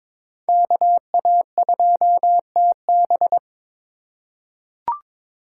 Morse code, heard as a keyed tone, about 22 wpm, 700 Hz. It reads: KA2TB
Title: Code Groups - 5 Letter:Numbers - 22wpm